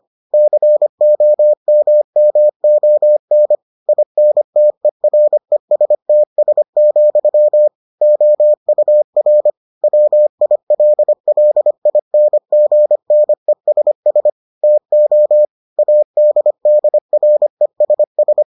Morse code, heard as a keyed tone, about 25 wpm, 600 Hz. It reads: COMMON INTERESTS, OUR WILLINGNESS TO ADDRESS